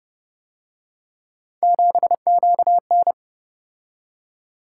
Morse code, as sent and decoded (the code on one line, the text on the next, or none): --... --.- -.
7QN